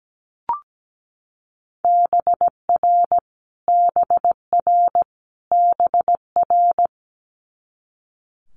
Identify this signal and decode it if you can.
Morse code, heard as a keyed tone, about 17 wpm, 700 Hz. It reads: BR BR BR
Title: Multi-letter Phonemes Repeated 3x 17wpm